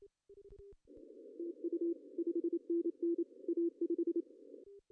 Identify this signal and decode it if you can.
Morse code, heard as a keyed tone, about 27 wpm, 340 Hz. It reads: TU 5NN A5
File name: CW / cqww-cw-k5tr